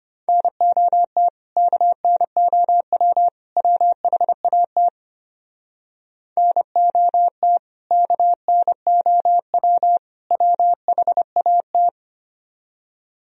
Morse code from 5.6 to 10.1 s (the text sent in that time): NOT KNOW